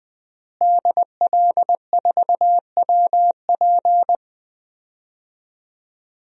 Morse code, heard as a keyed tone, about 20 wpm, 700 Hz. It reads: DL4WP
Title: DXCC-100 Call Signs - 5 Characters - Repeated 2x - 20wpm